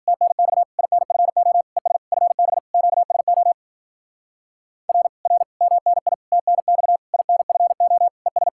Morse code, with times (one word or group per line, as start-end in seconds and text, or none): 0.1-0.6 TNX
0.8-1.6 INFO
1.8-2.0 ES
2.1-2.6 FB
2.7-3.5 QSO
4.9-5.1 R
5.3-5.4 R
5.6-6.1 MNI
6.3-7.0 TNX
7.1-8.1 INFO
8.3-8.5 ES